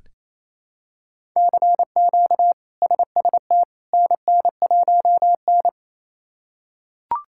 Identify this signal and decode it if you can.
Morse code, heard as a keyed tone, about 28 wpm, 700 Hz. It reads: CQ SST NN1N
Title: K1USN SST QSO Practice - 28wpm